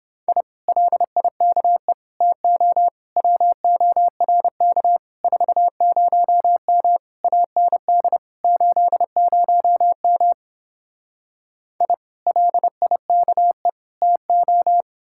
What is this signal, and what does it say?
Morse code, heard as a keyed tone, about 29 wpm, 700 Hz. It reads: I LIKE TO WORK 40M AND 80M I LIKE TO